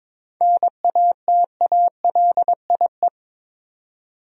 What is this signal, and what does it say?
Morse code, heard as a keyed tone, about 22 wpm, 700 Hz. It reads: NATALIE
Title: USA Common Names - 22wpm